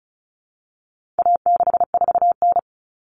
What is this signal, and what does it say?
Morse code, heard as a keyed tone, about 34 wpm, 700 Hz. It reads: A64N
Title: Common Cut Numbers - 4-Digit 35wpm